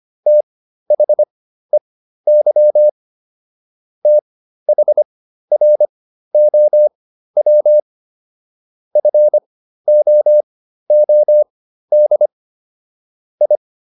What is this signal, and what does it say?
Morse code, heard as a keyed tone, about 25 wpm, spaced as for 13 wpm, 600 Hz. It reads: THEY THROW FOOD I